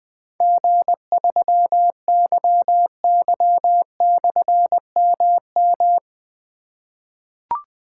Morse code, heard as a keyed tone, about 20 wpm, 700 Hz. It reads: G3YY/MM
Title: Difficult Call Signs 20wpm